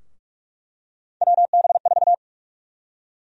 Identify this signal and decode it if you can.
Morse code, heard as a keyed tone, about 46 wpm, 700 Hz. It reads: WB4
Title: USA Call Sign Prefixes - Repeated 2x - 45wpm